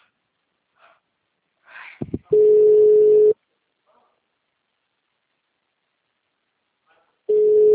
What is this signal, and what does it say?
Ringback tone (Europe, 425 Hz).
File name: out- -2002- - - .153